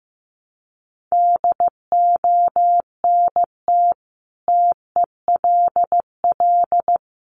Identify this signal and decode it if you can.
Morse code, heard as a keyed tone, about 14 wpm, 700 Hz. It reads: DONT TELL
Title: Sentences from Top 700 Words 15wpm